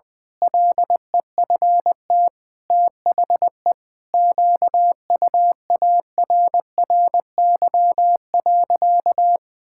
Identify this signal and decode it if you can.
Morse code, heard as a keyed tone, about 21 wpm, 700 Hz. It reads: LEFT THE QUARRY.
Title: Pangram Challenge v2 - 20wpm